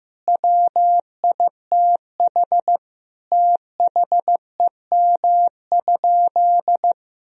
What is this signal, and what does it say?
Morse code, heard as a keyed tone, about 15 wpm, 700 Hz. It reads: WITH THEM?
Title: Sentences from Top 1000 Words - Review Part 2 - 15wpm